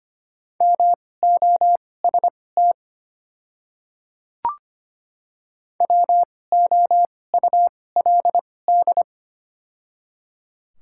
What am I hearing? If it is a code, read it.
Morse code, heard as a keyed tone, about 25 wpm, spaced as for 18 wpm, 690 Hz. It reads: MOST WOULD
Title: 2x Character Spacing - Top 500 Words - 25wpm